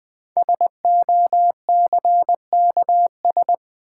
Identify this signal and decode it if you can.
Morse code, heard as a keyed tone, about 20 wpm, 700 Hz. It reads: SOCKS
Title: Easy Words with Hints - 20wpm